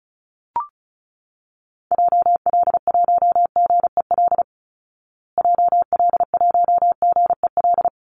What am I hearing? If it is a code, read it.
Morse code, heard as a keyed tone, about 34 wpm, 700 Hz. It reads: JL1GEL JL1GEL
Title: Call Signs Repeated 3x - 35wpm